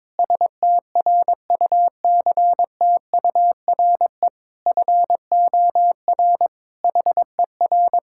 Morse code, heard as a keyed tone, about 22 wpm, 700 Hz. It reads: STRUCTURE FOR HER
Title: Sentences from Top 1000 Words - Review Part 6 - 22wpm